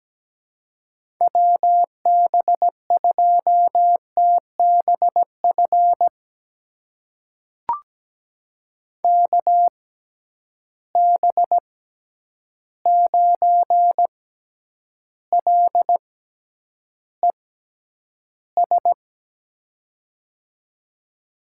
Morse code, heard as a keyed tone, about 17 wpm, spaced as for 9 wpm, 700 Hz. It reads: WB2TBF KB9LES
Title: USA Call Signs - 6 Characters - 6x Character Spacing - 17wpm